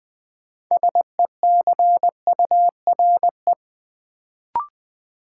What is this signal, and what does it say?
Morse code, heard as a keyed tone, about 20 wpm, 700 Hz. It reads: SECURE
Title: Sentences from Top 1200 Words - New Word Drill - 20wpm